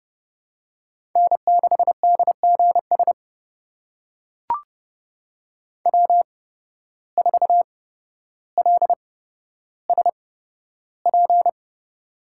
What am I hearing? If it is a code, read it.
Morse code, heard as a keyed tone, about 30 wpm, spaced as for 13 wpm, 700 Hz. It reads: N6DGS W4LSP
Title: USA Call Signs - 5 Characters - 8x Character Spacing - 30wpm